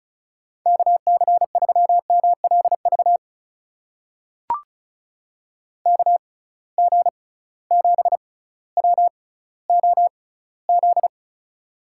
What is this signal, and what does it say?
Morse code, heard as a keyed tone, about 34 wpm, spaced as for 18 wpm, 700 Hz. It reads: KC3MLV KG7WOZ
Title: USA Call Signs - 6 Characters - 6x Character Spacing - 35wpm